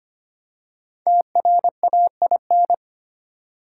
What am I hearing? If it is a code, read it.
Morse code, heard as a keyed tone, about 25 wpm, 700 Hz. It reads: TRAIN
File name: 5 Letter Words - First 3 Letters Given - 25wpm